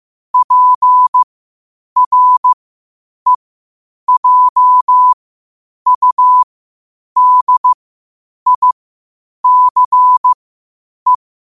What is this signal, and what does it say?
Morse code, heard as a keyed tone, about 15 wpm, spaced as for 8 wpm, 1000 Hz. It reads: PREJUDICE
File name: qotdfiles / 8wpm